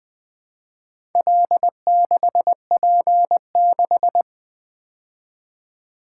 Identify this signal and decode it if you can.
Morse code, heard as a keyed tone, about 20 wpm, 700 Hz. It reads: L6P6
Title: Four Letter-Number Combinations - 20wpm